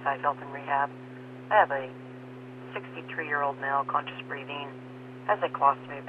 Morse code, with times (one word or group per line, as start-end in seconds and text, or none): none